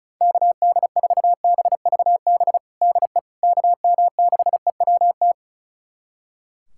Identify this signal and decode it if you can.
Morse code, heard as a keyed tone, about 34 wpm, 700 Hz. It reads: KD4BVB DE KM6EWT
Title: USA Callsign DE Callsign - 6 Characters - 35wpm